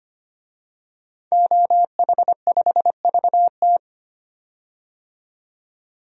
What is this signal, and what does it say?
Morse code, heard as a keyed tone, about 25 wpm, 700 Hz. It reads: OH5VT
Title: DXCC-100 Call Signs - 5 Characters - 25wpm